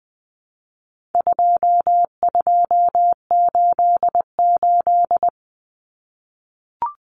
Morse code, heard as a keyed tone, about 20 wpm, 700 Hz. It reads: 2288
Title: Common Cut Numbers - 4-Digit 20wpm